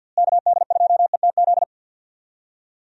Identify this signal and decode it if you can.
Morse code, heard as a keyed tone, about 52 wpm, 700 Hz. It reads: KD2ETZ
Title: USA Call Signs - 6 Characters - Repeated 3x - 50wpm